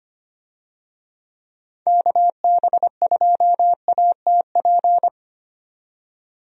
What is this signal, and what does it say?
Morse code, heard as a keyed tone, about 25 wpm, 700 Hz. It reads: KB2ATP
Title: USA Call Signs - 6 Characters - 25wpm